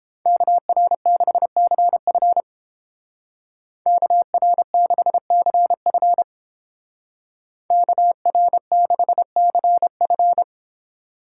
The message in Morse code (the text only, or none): KR6CF KR6CF KR6CF